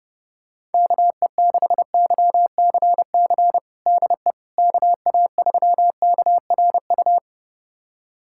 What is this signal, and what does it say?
Morse code, heard as a keyed tone, about 30 wpm, 700 Hz. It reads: KE6YCC DE KA3KRU
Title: USA Callsign DE Callsign 30wpm